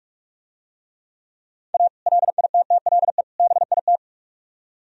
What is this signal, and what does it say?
Morse code, heard as a keyed tone, about 44 wpm, 700 Hz. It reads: A LITTLE BIT